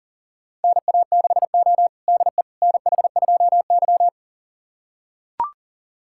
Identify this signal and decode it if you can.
Morse code, heard as a keyed tone, about 40 wpm, 700 Hz. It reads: NA6O DE NH2Y
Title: USA Callsign DE Callsign - 4 Characters - 40wpm